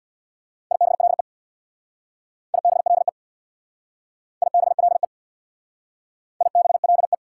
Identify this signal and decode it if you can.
Morse code, heard as a keyed tone, about 61 wpm, 690 Hz. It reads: IBLE IBLE IBLE IBLE
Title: Speed-Racing Prefixes and Suffixes 50wpm